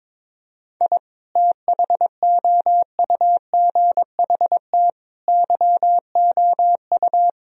Morse code, heard as a keyed tone, about 22 wpm, 700 Hz. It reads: I THOUGHT YOU
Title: Sentences from Top 1000 Words - Review Part 4 - 22wpm